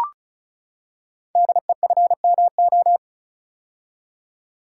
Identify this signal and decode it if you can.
Morse code, heard as a keyed tone, about 36 wpm, 700 Hz. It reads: DEFMO
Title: Code Groups - 5 Letters - 35wpm